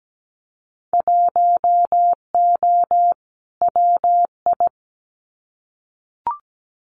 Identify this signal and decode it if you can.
Morse code, heard as a keyed tone, about 17 wpm, 700 Hz. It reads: 1O WI
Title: Winter Field Day Exchanges 17wpm